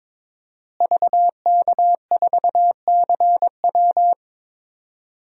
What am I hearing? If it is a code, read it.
Morse code, heard as a keyed tone, about 22 wpm, 700 Hz. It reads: VK4CW
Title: DXCC-100 Call Signs - 5 Characters - Repeated 2x - 22wpm